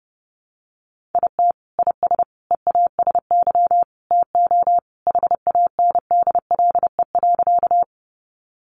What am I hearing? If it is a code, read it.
Morse code, heard as a keyed tone, about 30 wpm, 700 Hz. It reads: IT IS EASY TO HANDLE.
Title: Sentences from Top 1100 Words 30wpm